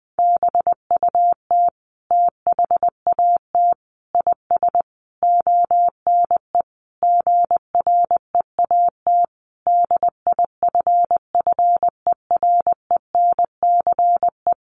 Morse code, heard as a keyed tone, about 20 wpm, 700 Hz. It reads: BUT THAT IS ONE GREAT DIFFERENCE
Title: Sentences from Top 800 Words 20wpm